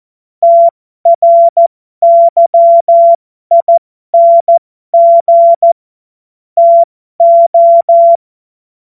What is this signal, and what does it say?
Morse code, heard as a keyed tone, about 14 wpm, spaced as for 12 wpm, 680 Hz. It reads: TRYING TO